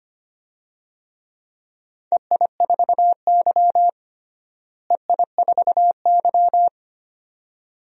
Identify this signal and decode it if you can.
Morse code, heard as a keyed tone, about 25 wpm, 700 Hz. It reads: EI4Y EI4Y
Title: DXCC-100 Call Signs - 4 Characters - Repeated 2x - 25wpm